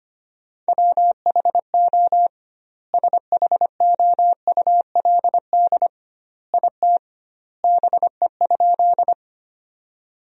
Morse code, heard as a keyed tone, about 25 wpm, 700 Hz. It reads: WHO SHOULD IT BE?
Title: 2x Word-Spacing - Sentences from Top 300 Words - 25wpm